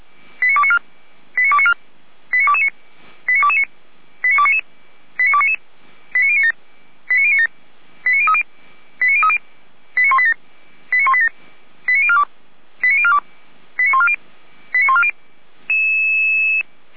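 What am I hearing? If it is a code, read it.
Five-tone selective call (ZVEI), heard as quick runs of five tones, tones of about 70 ms. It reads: 89294 89294 89229 89229 89299 89299 89097 89097 89030 89030 89187 89187 89042 89042 89140 89140
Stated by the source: ZVEI five-tone selective call received by SDR, hosted by Feuerwehr Orlamünde.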